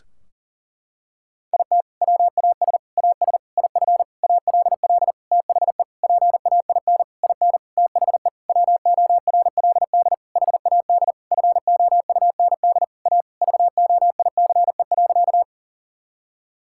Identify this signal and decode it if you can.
Morse code, heard as a keyed tone, about 40 wpm, 700 Hz. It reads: IT WAS AS IF ALL THE PAIN IN THE WORLD HAD FOUND A VOICE.
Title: Sentences from Top 1600 Words 40wpm